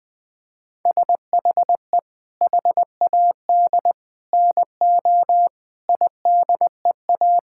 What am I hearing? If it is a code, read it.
Morse code, heard as a keyed tone, about 21 wpm, 700 Hz. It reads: SHE HAD NO IDEA
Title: Sentences with 7-Word Phrases - 20wpm